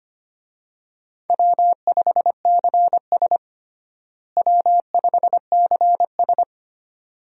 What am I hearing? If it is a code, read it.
Morse code, heard as a keyed tone, about 25 wpm, 700 Hz. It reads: W5CS W5CS
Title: USA Call Signs - 4 Characters - Repeated 2x - 25wpm